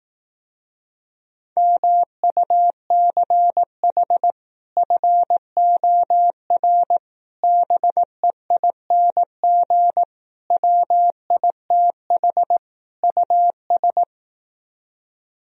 Morse code, heard as a keyed tone, about 18 wpm, 700 Hz. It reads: MUCH FOR BEING WITH US
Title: Speed-Racing - Sets of 5 Words 15wpm